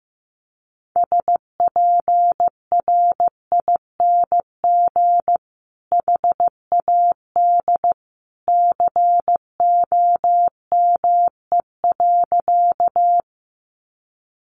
Morse code, heard as a keyed tone, about 15 wpm, 700 Hz. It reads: SPRING HAD COME.
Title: Sentences from Top 1200 Words 15wpm